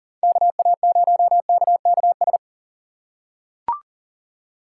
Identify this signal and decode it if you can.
Morse code, heard as a keyed tone, about 39 wpm, 700 Hz. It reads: KA0KKS
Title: USA Call Signs - 6 Characters - 6x Character Spacing - 40wpm